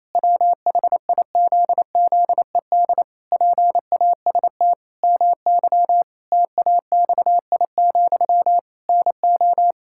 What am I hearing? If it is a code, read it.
Morse code, heard as a keyed tone, about 28 wpm, 700 Hz. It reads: WHIZZED PAST MY TAXI, NO
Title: Pangram Challenge v2 - 28wpm